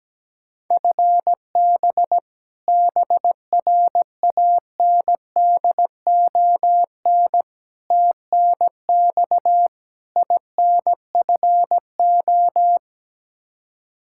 Morse code, heard as a keyed tone, about 17 wpm, 700 Hz. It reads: FB BRANDON TNX INFO
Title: Practice QSO - Exchange 3 - More Info - 17wpm